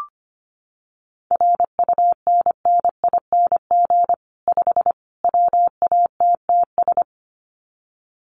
Morse code, heard as a keyed tone, about 25 wpm, 700 Hz. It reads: RUNNING 5 WATTS
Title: Common QSO Phrases 25wpm